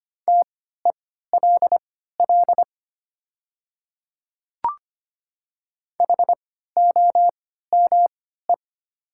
Morse code, heard as a keyed tone, about 25 wpm, spaced as for 14 wpm, 700 Hz. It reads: TELL HOME
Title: 3x Character Spacing - Top 300 Words - 25wpm